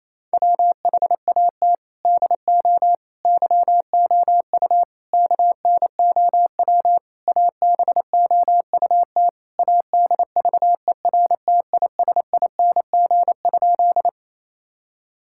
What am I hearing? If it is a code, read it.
Morse code, heard as a keyed tone, about 27 wpm, 700 Hz. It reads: WHAT DO YOU KNOW ABOUT ADVERTISING?
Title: Sentences from Top 1000 Words - Review Part 6 - 28wpm